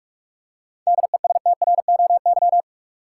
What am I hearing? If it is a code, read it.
Morse code, heard as a keyed tone, about 47 wpm, 700 Hz. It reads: DESTROY
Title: Seven Letter Words - 45wpm